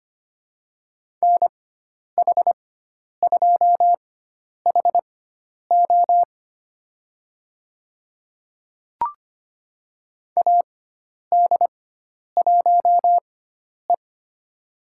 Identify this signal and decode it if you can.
Morse code, heard as a keyed tone, about 25 wpm, spaced as for 10 wpm, 700 Hz. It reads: NH2HO AD1E
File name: Lesson 16g - Letter c - Call Signs 25wpm